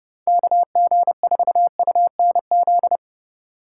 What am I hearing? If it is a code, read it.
Morse code, heard as a keyed tone, about 37 wpm, 700 Hz. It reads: KG4UNZ
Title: Speed Racing - USA Call Signs 30wpm